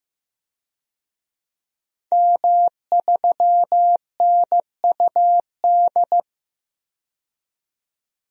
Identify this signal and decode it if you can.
Morse code, heard as a keyed tone, about 15 wpm, 700 Hz. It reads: M3NUD
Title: DXCC-100 Call Signs - 5 Characters - 15wpm